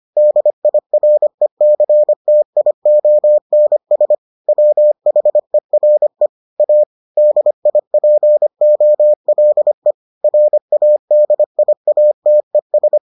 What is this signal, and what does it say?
Morse code, heard as a keyed tone, about 25 wpm, 600 Hz. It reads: DIRECTIONS WHERE A DIPOLE RADIATES